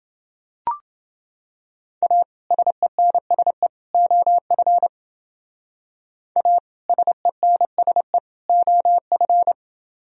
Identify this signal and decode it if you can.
Morse code, heard as a keyed tone, about 29 wpm, 700 Hz. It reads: A SENSE OF A SENSE OF